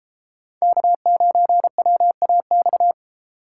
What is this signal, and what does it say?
Morse code, heard as a keyed tone, about 30 wpm, 700 Hz. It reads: K9WAX